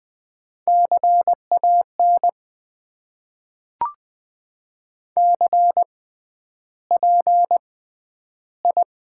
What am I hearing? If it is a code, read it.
Morse code, heard as a keyed tone, about 20 wpm, spaced as for 10 wpm, 700 Hz. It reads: CAN CPI
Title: Top 90 QSO Elements - 6x Character Spacing - 20wpm